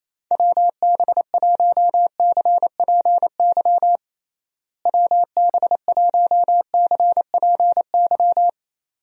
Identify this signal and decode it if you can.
Morse code, heard as a keyed tone, about 28 wpm, 700 Hz. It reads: WB1CPY WB1CPY
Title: USA Call Signs - 6 Characters - Repeated 2x - 28wpm